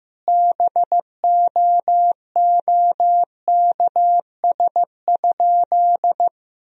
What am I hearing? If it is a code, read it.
Morse code, heard as a keyed tone, about 15 wpm, 700 Hz. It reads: BOOKS?
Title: Sentences from Top 1000 Words - Review Part 4 - 15wpm